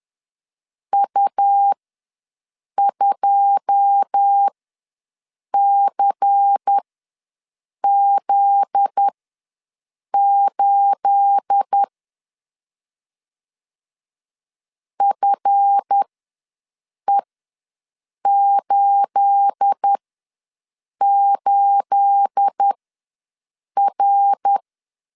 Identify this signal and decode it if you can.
Morse code, heard as a keyed tone, about 10 wpm, spaced as for 6 wpm, 800 Hz. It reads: U2CZ8 FE88R
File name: LEZIONI / 9 28 ws 10wpm 4124